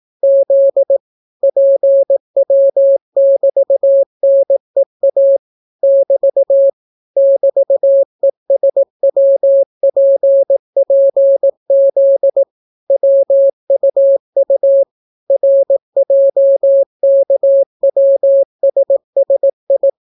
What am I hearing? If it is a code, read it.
Morse code, heard as a keyed tone, about 18 wpm, 550 Hz. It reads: Z PW<BT>NEA <BT> <BT>ESWPPZ WUU RJKWSSI